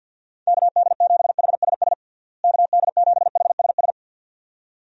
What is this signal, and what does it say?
Morse code, heard as a keyed tone, about 52 wpm, 700 Hz. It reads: KD7HSS KD7HSS
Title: USA Call Signs - 6 Characters - Repeated 2x - 50wpm